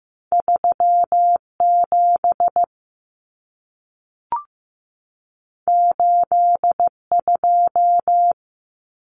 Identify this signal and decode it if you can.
Morse code, heard as a keyed tone, about 15 wpm, 700 Hz. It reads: 3 7 8 2
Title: Numbers - 2-Digit 15wpm